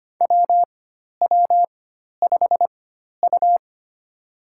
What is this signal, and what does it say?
Morse code, heard as a keyed tone, about 25 wpm, spaced as for 12 wpm, 700 Hz. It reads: WW5U
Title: USA Call Signs - 4 Characters - 4x Character Spacing - 25wpm